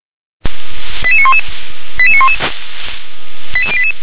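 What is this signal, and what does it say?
Five-tone selective call (ZVEI), heard as quick runs of five tones, tones of about 70 ms. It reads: 88911 88911 88088
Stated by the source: ZVEI five-tone call received by SDR.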